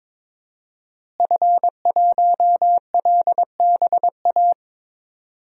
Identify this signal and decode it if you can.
Morse code, heard as a keyed tone, about 22 wpm, 700 Hz. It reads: F1LBA
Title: DXCC-100 Call Signs - 5 Characters - Repeated 2x - 22wpm